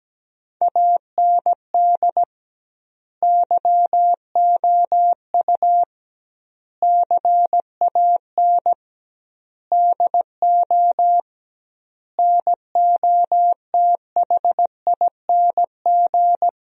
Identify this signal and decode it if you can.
Morse code, heard as a keyed tone, about 17 wpm, 700 Hz. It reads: AND YOU CAN DO NOTHING
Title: 2x Word-Spacing - Sentences from Top 500 Words - 17wpm